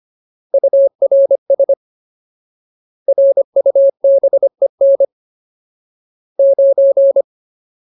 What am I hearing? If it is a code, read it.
Morse code, heard as a keyed tone, about 25 wpm, 550 Hz. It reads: URS RUBEN 9
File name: Ultimate Word List / 25WPM Ultimate